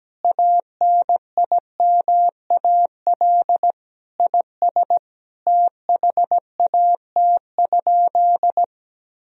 Morse code, heard as a keyed tone, about 17 wpm, 700 Hz. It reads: ANIMAL IS THAT?
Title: Sentences from Top 1000 Words - Review Part 1 - 17wpm